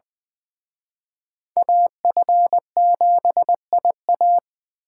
Morse code, heard as a keyed tone, about 24 wpm, 700 Hz. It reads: AF7IA
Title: Speed Racing - USA Call Signs 20wpm